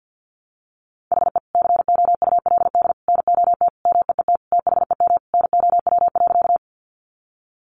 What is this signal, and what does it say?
Morse code, heard as a keyed tone, about 49 wpm, 700 Hz. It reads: HE COULD NOT MEET THEM NOW.
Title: Sentences from Top 1100 Words 50wpm